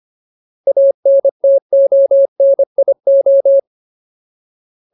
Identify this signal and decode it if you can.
Morse code, heard as a keyed tone, about 25 wpm, 550 Hz. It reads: ANTONIO
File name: Ultimate Word List / 25WPM Ultimate